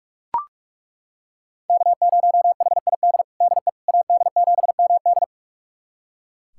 Morse code, heard as a keyed tone, about 42 wpm, 700 Hz. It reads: K0HID DE AD7MD
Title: USA Callsign DE Callsign - 5 Characters - 45wpm